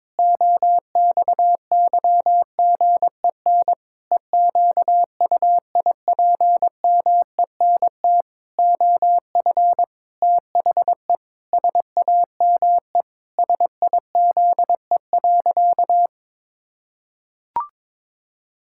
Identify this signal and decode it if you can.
Morse code, heard as a keyed tone, about 21 wpm, 700 Hz. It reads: OXYGEN EQUIPMENT OF THE SAME SIZE.
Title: Pangram Challenge v2 - 22wpm